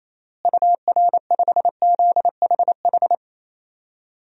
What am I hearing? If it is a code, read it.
Morse code, heard as a keyed tone, about 27 wpm, 700 Hz. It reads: UR5ZHH
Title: DXCC-100 Call Signs - 6 Characters - Repeated 3x - 28wpm